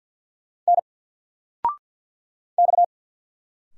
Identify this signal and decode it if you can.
Morse code, heard as a keyed tone, about 48 wpm, 700 Hz. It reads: N X
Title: Single Character (Letter:Number:Punctuation:Prosign) 50wpm